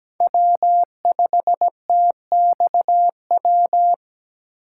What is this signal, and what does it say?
Morse code, heard as a keyed tone, about 17 wpm, 700 Hz. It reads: W5TXW
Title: USA Call Signs - 5 Characters - 8x Character Spacing - 17wpm